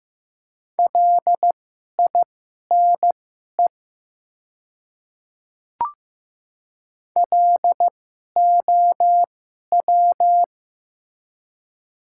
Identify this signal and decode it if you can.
Morse code, heard as a keyed tone, about 15 wpm, spaced as for 11 wpm, 700 Hz. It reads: LINE LOW
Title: 2x Character Spacing - Top 500 Words - 15wpm